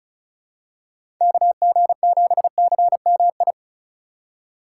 Morse code, heard as a keyed tone, about 36 wpm, 700 Hz. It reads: KG7CMI
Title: USA Call Signs - 6 Characters - 3x Character Spacing - 35wpm